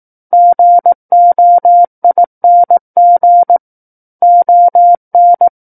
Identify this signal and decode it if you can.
Morse code, heard as a keyed tone, about 18 wpm, 700 Hz. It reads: GOING ON
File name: 18 WPM